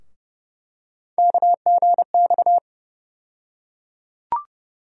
Morse code, heard as a keyed tone, about 31 wpm, 700 Hz. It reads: KGX
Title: USA Call Sign Postfixes - 3 Letters 30wpm